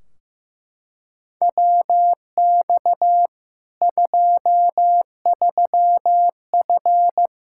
Morse code, heard as a keyed tone, about 15 wpm, 700 Hz. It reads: WX 23F
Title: K1USN SST QSO Practice 15wpm